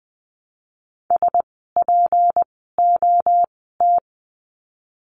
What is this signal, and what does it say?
Morse code, heard as a keyed tone, about 20 wpm, spaced as for 15 wpm, 700 Hz. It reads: SPOT
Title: 153 Lesson 25f - Letter p - Words 20wpm